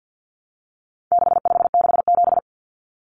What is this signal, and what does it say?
Morse code, heard as a keyed tone, about 50 wpm, 700 Hz. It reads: 6567
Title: Numbers - 4-Digit 50wpm